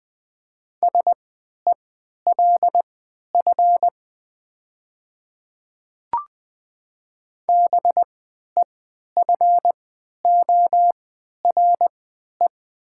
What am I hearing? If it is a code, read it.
Morse code, heard as a keyed tone, about 20 wpm, spaced as for 11 wpm, 700 Hz. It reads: SELF BEFORE
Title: 3x Character Spacing - Top 300 Words - 20wpm